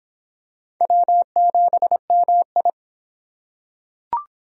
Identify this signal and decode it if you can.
Morse code, heard as a keyed tone, about 22 wpm, 700 Hz. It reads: W7MI